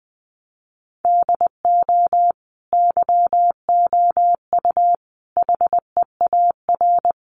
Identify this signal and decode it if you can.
Morse code, heard as a keyed tone, about 20 wpm, 700 Hz. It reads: DO YOU HEAR
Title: Sentences from Top 300 Words 20wpm